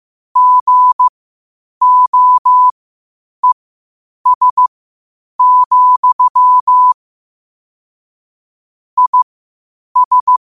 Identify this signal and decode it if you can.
Morse code, heard as a keyed tone, about 15 wpm, spaced as for 8 wpm, 1000 Hz. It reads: GOES, IS
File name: qotdfiles / 8wpm